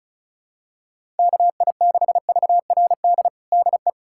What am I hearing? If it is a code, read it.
Morse code, heard as a keyed tone, about 36 wpm, 700 Hz. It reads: KI6VRD DE
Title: USA Callsign DE Callsign 35wpm